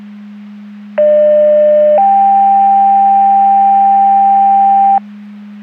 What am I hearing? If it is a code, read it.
Two-tone page, heard as a one-second tone, then a three-second tone, about 600 then 790 Hz, with noise. It is heard from 1.0 to 5.0 s.
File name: Portage 154.385 Paging